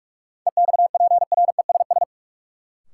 Morse code, heard as a keyed tone, about 46 wpm, 700 Hz. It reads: EXPRESS
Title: Sentences from Top 1100 Words - New Word Drill - 45wpm